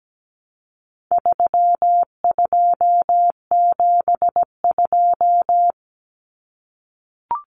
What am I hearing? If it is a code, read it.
Morse code, heard as a keyed tone, about 17 wpm, 700 Hz. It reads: 3272
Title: Common Cut Numbers - 4-Digit 17wpm